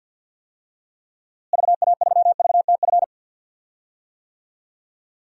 Morse code, heard as a keyed tone, about 53 wpm, 700 Hz. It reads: VA3VTF
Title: DXCC-100 Call Signs - 6 Characters - Repeated 2x - 50wpm